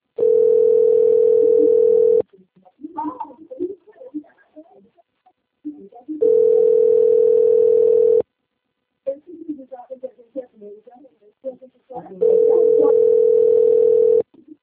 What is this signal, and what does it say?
Ringback tone (North America), 440 + 480 Hz, with speech.
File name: rg-1600-unknown- - - .33200